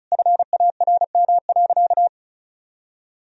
Morse code, heard as a keyed tone, about 34 wpm, 700 Hz. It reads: FARM.
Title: Sentences from Top 1000 Words - Review Part 2 - 35wpm